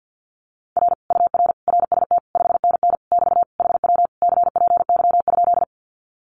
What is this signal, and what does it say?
Morse code, heard as a keyed tone, about 48 wpm, 700 Hz. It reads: R UR RST 5NN <BT> HW CPY?
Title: Common QSO Phrases Encore 50wpm